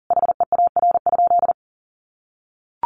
Morse code, heard as a keyed tone, about 40 wpm, 700 Hz. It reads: HEAR?
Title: Sentences from Top 300 Words 40wpm